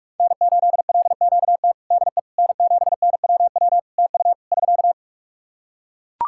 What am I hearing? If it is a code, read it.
Morse code, heard as a keyed tone, about 42 wpm, 700 Hz. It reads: N8LQT DE N7NWW TU <SK>
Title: Practice QSO - Exchange 4 - Ending - 45wpm